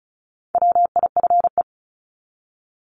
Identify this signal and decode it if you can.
Morse code, heard as a keyed tone, about 34 wpm, 700 Hz. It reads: WIFE
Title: Top 1751-2000 Words 35wpm